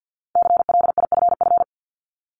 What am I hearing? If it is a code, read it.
Morse code, heard as a keyed tone, about 50 wpm, 700 Hz. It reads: CLIFF
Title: Multi-letter Phonemes 50wpm